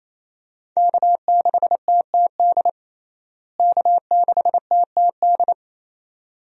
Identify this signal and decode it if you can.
Morse code, heard as a keyed tone, about 28 wpm, 700 Hz. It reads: K6TTD K6TTD
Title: USA Call Signs - 5 Characters - Repeated 2x - 28wpm